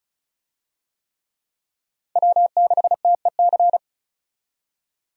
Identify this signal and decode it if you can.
Morse code, heard as a keyed tone, about 34 wpm, 700 Hz. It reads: W6TEC
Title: USA Call Signs - 5 Characters - 35wpm